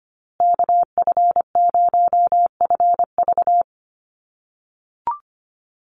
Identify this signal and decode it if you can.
Morse code, heard as a keyed tone, about 25 wpm, 700 Hz. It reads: KF0FV
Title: USA Call Signs Repeated 3x 25wpm